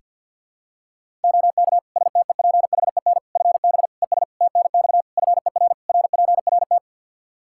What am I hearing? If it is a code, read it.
Morse code, heard as a keyed tone, about 48 wpm, 700 Hz. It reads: OK STEPHEN FB ES TNX FER RPRT
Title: Practice QSO - Exchange 2 - Info - 50wpm